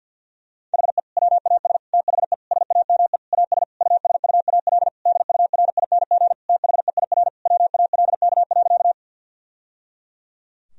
Morse code, heard as a keyed tone, about 48 wpm, 700 Hz. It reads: HE WAS THE SAME AS USUAL DURING THEIR WALK.
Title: Sentences from Top 1000 Words - Review Part 2 - 50wpm